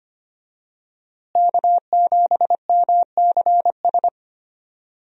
Morse code, heard as a keyed tone, about 25 wpm, 700 Hz. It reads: K7MCS
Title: USA Call Signs - 5 Characters - Repeated 3x - 25wpm